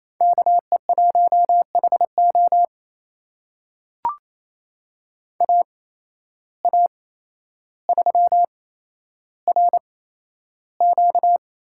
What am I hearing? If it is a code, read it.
Morse code, heard as a keyed tone, about 28 wpm, spaced as for 12 wpm, 700 Hz. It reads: KE1HO AA3RQ
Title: USA Call Signs - 5 Characters - 8x Character Spacing - 28wpm